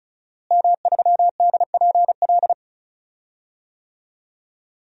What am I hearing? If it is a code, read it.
Morse code, heard as a keyed tone, about 36 wpm, 700 Hz. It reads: M3DPL
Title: DXCC-100 Call Signs - 5 Characters - Repeated 2x - 35wpm